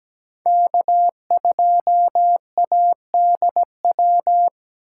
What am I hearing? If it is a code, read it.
Morse code, heard as a keyed tone, about 20 wpm, 700 Hz. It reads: K2ADW
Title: Speed Racing - USA 1x3 Call Signs 17wpm